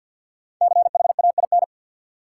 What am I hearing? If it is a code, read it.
Morse code, heard as a keyed tone, about 52 wpm, 700 Hz. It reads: CHAIN
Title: Five Letter Words - 50wpm